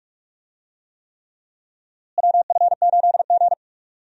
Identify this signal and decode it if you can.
Morse code, heard as a keyed tone, about 43 wpm, 700 Hz. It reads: WF8G
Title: USA Call Signs - 4 Characters - Repeated 3x - 45wpm